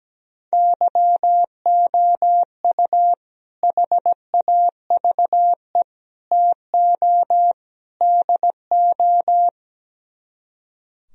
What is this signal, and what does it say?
Morse code, heard as a keyed tone, about 17 wpm, 700 Hz. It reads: YOU HAVE TO DO
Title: Speed-Racing - Sets of 4 Words 17wpm